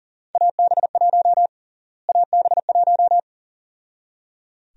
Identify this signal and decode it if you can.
Morse code, heard as a keyed tone, about 37 wpm, 700 Hz. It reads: AB1 AB1
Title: USA Call Sign Prefixes - Repeated 2x - 40wpm